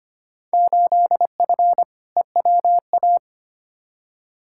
Morse code, heard as a keyed tone, about 25 wpm, 700 Hz. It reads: 8F EWA
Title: Field Day Exchanges Encore - 25wpm